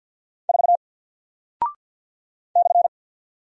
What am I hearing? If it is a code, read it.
Morse code, heard as a keyed tone, about 50 wpm, 700 Hz. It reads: 4 /
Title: Single Character (Letter:Number:Punctuation:Prosign) Alt 50wpm